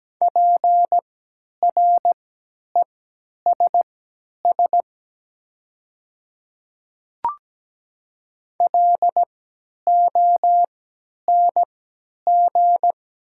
Morse code, heard as a keyed tone, about 17 wpm, spaced as for 10 wpm, 700 Hz. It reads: PRESS LONG
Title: 3x Character Spacing - Top 300 Words - 17wpm